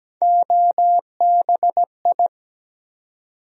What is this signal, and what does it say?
Morse code, heard as a keyed tone, about 17 wpm, 700 Hz. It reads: OBI
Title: USA Call Sign Postfixes - 3 Letters 17wpm